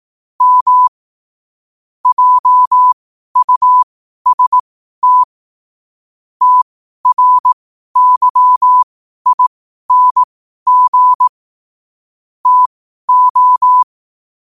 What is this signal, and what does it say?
Morse code, heard as a keyed tone, about 18 wpm, spaced as for 12 wpm, 1000 Hz. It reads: M JUST TRYING TO
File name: qotdfiles / 12wpm